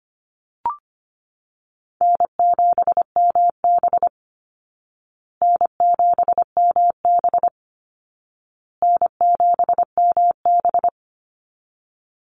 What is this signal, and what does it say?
Morse code, heard as a keyed tone, about 25 wpm, 700 Hz. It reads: N7MB N7MB N7MB
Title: Easy Call Signs Repeated 3x - 25wpm